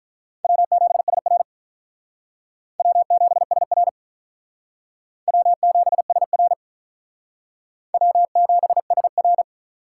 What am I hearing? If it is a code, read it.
Morse code, heard as a keyed tone, about 44 wpm, 700 Hz. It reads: W7SR W7SR W7SR W7SR